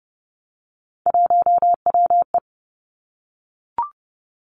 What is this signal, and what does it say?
Morse code, heard as a keyed tone, about 30 wpm, 700 Hz. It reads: 1WE
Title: Three Challenging Letter-Number Combinations 30wpm